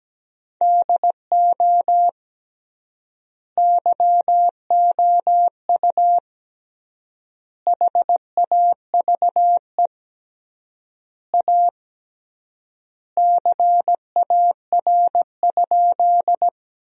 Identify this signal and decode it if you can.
Morse code, heard as a keyed tone, about 17 wpm, 700 Hz. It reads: DO YOU HAVE A CAR?
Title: 3x Word-Spacing - Sentences from Top 300 Words - 17wpm